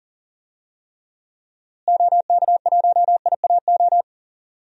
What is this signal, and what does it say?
Morse code, heard as a keyed tone, about 37 wpm, 700 Hz. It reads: OK1IAO
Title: DXCC-100 Call Signs - 6 Characters - Repeated 3x - 40wpm